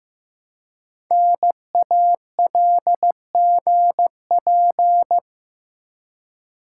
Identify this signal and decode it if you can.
Morse code, heard as a keyed tone, about 15 wpm, 700 Hz. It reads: NALGP
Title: Code Groups - 5 Letters - 15wpm